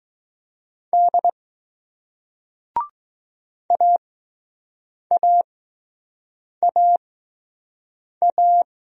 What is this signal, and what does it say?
Morse code, heard as a keyed tone, about 20 wpm, spaced as for 6 wpm, 700 Hz. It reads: D AAAA